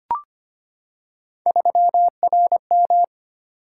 Morse code, heard as a keyed tone, about 25 wpm, 700 Hz. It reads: 3RM
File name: Three Letter-Number Combinations - 25wpm